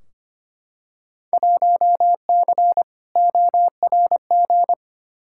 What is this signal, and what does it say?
Morse code, heard as a keyed tone, about 25 wpm, 700 Hz. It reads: 1C ORG
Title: Field Day Exchanges - 25wpm